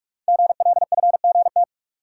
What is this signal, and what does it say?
Morse code, heard as a keyed tone, about 44 wpm, 690 Hz. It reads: GLFGT
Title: Code Groups - 5 Letters - 45wpm